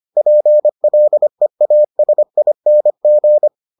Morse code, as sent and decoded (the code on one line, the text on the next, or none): .--. .-.. .- ... .. -. --.
PLEASING